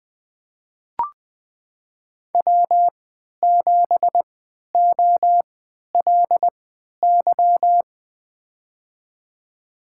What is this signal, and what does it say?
Morse code, heard as a keyed tone, about 20 wpm, spaced as for 11 wpm, 700 Hz. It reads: W7OLY
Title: USA Call Signs - 3x Character Spacing - 20wpm